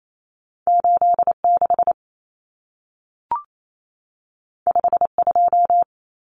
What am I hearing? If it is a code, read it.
Morse code, heard as a keyed tone, about 28 wpm, 700 Hz. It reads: 86 52
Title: Common Cut Numbers - 2-Digit 28wpm